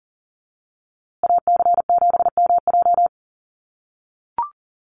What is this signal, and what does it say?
Morse code, heard as a keyed tone, about 40 wpm, 700 Hz. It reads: AC7MJ
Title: USA Call Signs Repeated 2x 40wpm